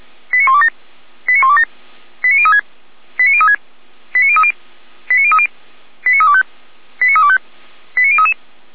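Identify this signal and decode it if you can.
Five-tone selective call (ZVEI), heard as quick runs of five tones, tones of about 70 ms. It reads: 89127 89127 89036 89036 89030 89030 89325 89325 89033
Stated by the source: ZVEI five-tone call received by SDR, archived by Feuerwehr Orlamünde.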